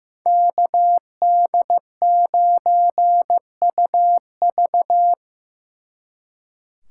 Morse code, heard as a keyed tone, about 18 wpm, 700 Hz. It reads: KD9UV
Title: Speed Racing - USA 2x2 Call Signs 15wpm